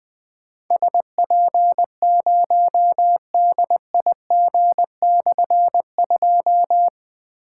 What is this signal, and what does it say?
Morse code, heard as a keyed tone, about 21 wpm, 700 Hz. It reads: SP0DIG/2
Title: Difficult Call Signs 20wpm